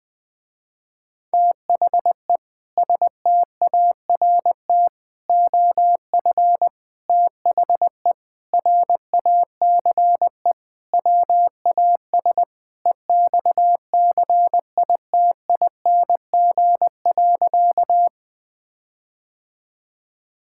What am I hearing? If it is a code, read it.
Morse code, heard as a keyed tone, about 21 wpm, 700 Hz. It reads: THE START OF THE RACE WAS EXCITING.
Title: Sentences with 4-Word Phrases - Spoken Beginning - 20wpm